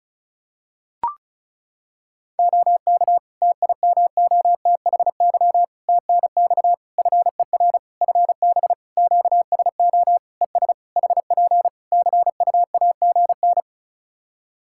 Morse code, heard as a keyed tone, about 36 wpm, 700 Hz. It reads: OK TIMOTHY TNX FER FB QSO ES HP CUAGN
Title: K1USN SST QSO Practice 35wpm